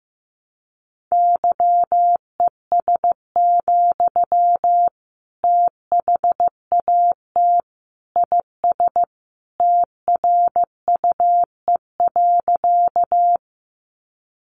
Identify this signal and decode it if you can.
Morse code, heard as a keyed tone, about 15 wpm, 700 Hz. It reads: YES, THAT IS TRUE.
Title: Sentences from Top 500 Words 15wpm